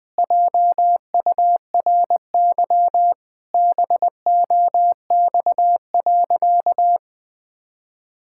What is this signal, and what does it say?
Morse code, heard as a keyed tone, about 20 wpm, 700 Hz. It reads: JURY BOX.
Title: Pangram Challenge v2 - 20wpm